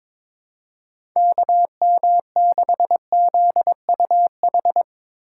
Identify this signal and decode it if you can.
Morse code, heard as a keyed tone, about 22 wpm, 700 Hz. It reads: KM6ZUH
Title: USA Call Signs - 6 Characters - 2x Character Spacing - 22wpm